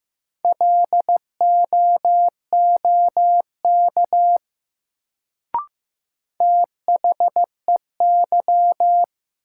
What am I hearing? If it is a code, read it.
Morse code, heard as a keyed tone, about 15 wpm, 700 Hz. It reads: LOOK THEY
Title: Sentences from Top 100 Words - New Word Drill - 15wpm